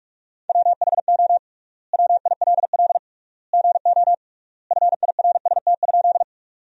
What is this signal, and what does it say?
Morse code, heard as a keyed tone, about 45 wpm, 700 Hz. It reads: WHO WILL GO FIRST?
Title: 3x Word-Spacing - Sentences from Top 100 Words - 45wpm